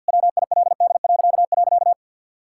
Morse code, heard as a keyed tone, about 50 wpm, 700 Hz. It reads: WILD..
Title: Pangram Challenge v2 - 50wpm